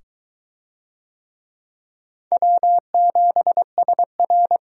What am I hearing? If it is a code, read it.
Morse code, heard as a keyed tone, about 22 wpm, 700 Hz. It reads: W7SR